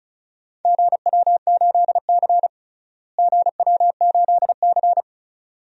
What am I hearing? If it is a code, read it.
Morse code, heard as a keyed tone, about 36 wpm, 700 Hz. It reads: GW8C GW8C
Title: DXCC-100 Call Signs - 4 Characters - Repeated 2x - 35wpm